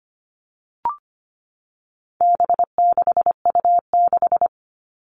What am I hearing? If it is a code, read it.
Morse code, heard as a keyed tone, about 25 wpm, 700 Hz. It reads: B6U6
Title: All Cut Numbers - 4-Digit 25wpm